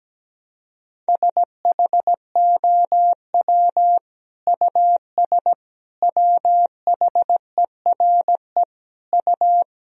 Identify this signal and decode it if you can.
Morse code, heard as a keyed tone, about 17 wpm, 700 Hz. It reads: SHOW US WHERE U
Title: Sentences from Top 200 Words and Abbreviations 17wpm